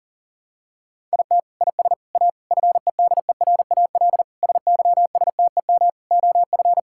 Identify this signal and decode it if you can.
Morse code, heard as a keyed tone, about 40 wpm, 700 Hz. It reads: IT IS A FEDERAL SYSTEM OF
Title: Sentences from Top 1000 Words - Review Part 5 - 40wpm